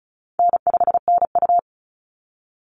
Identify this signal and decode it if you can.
Morse code, heard as a keyed tone, about 34 wpm, 700 Hz. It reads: N5NU
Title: Call Signs - 35wpm